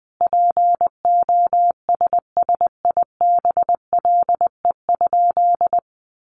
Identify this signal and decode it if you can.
Morse code, heard as a keyed tone, about 20 wpm, 700 Hz. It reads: POSSIBLE?
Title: Sentences from Top 500 Words 20wpm